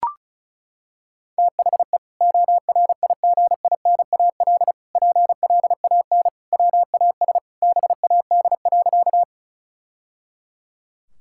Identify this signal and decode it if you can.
Morse code, heard as a keyed tone, about 36 wpm, 700 Hz. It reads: THE ORIGINAL PLAN WAS BAD.
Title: Sentences from Top 1000 Words - Review Part 6 - 35wpm